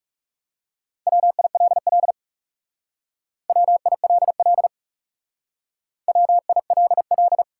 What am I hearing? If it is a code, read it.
Morse code, heard as a keyed tone, about 37 wpm, 700 Hz. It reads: WILL WILL WILL
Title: Speed-Racing Top 180 QSO Elements 30wpm